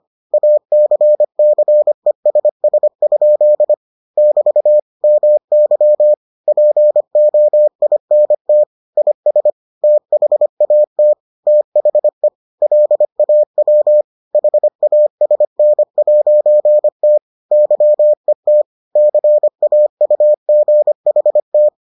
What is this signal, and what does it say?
Morse code, heard as a keyed tone, about 25 wpm, 600 Hz. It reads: ACCESS? <BT> MY POINT IS THAT THE LAW HASN'T YET CAUGHT